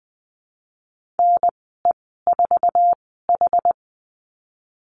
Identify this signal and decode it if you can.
Morse code, heard as a keyed tone, about 20 wpm, spaced as for 14 wpm, 700 Hz. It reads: NE4H
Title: 088 Lesson 15g - Number 5 - Call Signs 20wpm